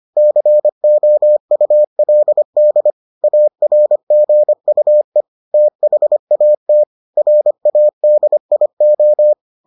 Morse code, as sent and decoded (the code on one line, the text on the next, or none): -.-. --- ..- .-.. -.. / .- .-. --. ..- . / - .... .- - / .-. .- -.. .. ---
COULD ARGUE THAT RADIO